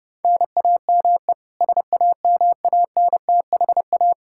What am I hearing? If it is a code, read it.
Morse code, heard as a keyed tone, about 32 wpm, 700 Hz. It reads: NAME SAMANTHA
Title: Practice QSO - Exchange 1 - Beginning - 30wpm